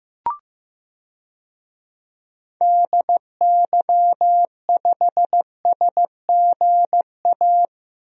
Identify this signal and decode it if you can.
Morse code, heard as a keyed tone, about 15 wpm, 700 Hz. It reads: DY5SGA
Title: DXCC-100 Call Signs - 6 Characters - Repeated 2x - 15wpm